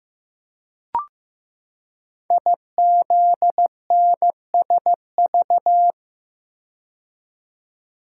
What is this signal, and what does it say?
Morse code, heard as a keyed tone, about 15 wpm, 700 Hz. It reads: IZNSV
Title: Code Groups - 5 Letters - 15wpm